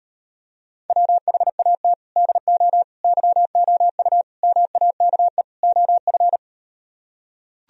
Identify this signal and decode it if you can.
Morse code, heard as a keyed tone, about 38 wpm, 700 Hz. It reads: WHAT DO YOU MAKE OF